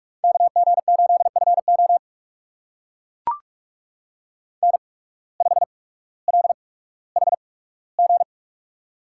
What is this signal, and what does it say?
Morse code, heard as a keyed tone, about 43 wpm, spaced as for 19 wpm, 700 Hz. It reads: KG8FO N5LHG
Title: USA Call Signs - 5 Characters - 8x Character Spacing - 45wpm